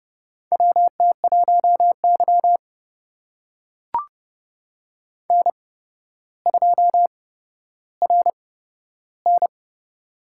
Morse code, heard as a keyed tone, about 30 wpm, spaced as for 13 wpm, 700 Hz. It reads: WT1Y N2RN
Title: USA Call Signs - 4 Characters - 8x Character Spacing - 30wpm